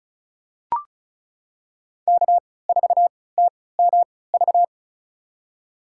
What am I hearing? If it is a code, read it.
Morse code, heard as a keyed tone, about 36 wpm, spaced as for 20 wpm, 700 Hz. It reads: K4TMV
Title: USA Call Signs - 5 Characters - 3x Character Spacing - 35wpm